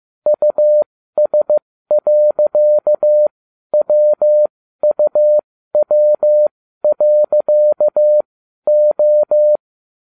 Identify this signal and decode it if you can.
Morse code, heard as a keyed tone, about 15 wpm, spaced as for 12 wpm, 600 Hz. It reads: US.WUW.O